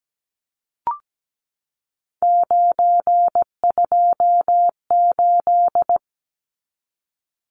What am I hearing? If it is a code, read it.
Morse code, heard as a keyed tone, about 17 wpm, 700 Hz. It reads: 928
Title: Numbers - 3-Digit 17wpm